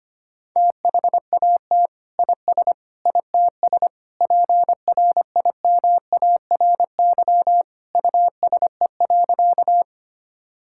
Morse code, heard as a keyed tone, about 25 wpm, 700 Hz. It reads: THAT IS ITS PRIMARY USE.
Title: Sentences from Top 1000 Words - Review Part 5 - 25wpm